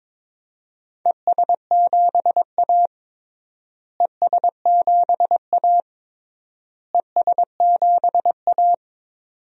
Morse code, heard as a keyed tone, about 22 wpm, 700 Hz. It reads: ES7A ES7A ES7A
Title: DXCC-100 Call Signs - 4 Characters - Repeated 3x - 22wpm